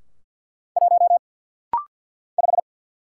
Morse code, heard as a keyed tone, about 46 wpm, 700 Hz. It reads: 1 5
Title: Single Letter-Number - Rapid-Fire 50wpm